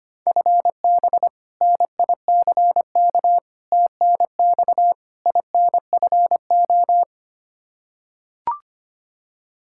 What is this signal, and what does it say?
Morse code, heard as a keyed tone, about 25 wpm, 700 Hz. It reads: FB NICK TNX INFO
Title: Practice QSO - Exchange 3 - More Info - 25wpm